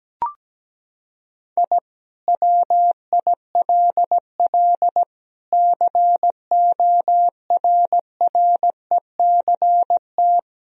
Morse code, heard as a keyed tone, about 17 wpm, 700 Hz. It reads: I WILL CORRECT
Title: Sentences from Top 1000 Words - Review Part 2 - 17wpm